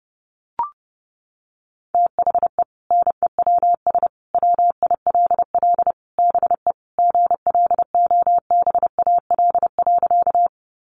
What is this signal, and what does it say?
Morse code, heard as a keyed tone, about 28 wpm, 700 Hz. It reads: THE NEWS WILL BE GLOBAL.
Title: Sentences from Top 1200 Words 30wpm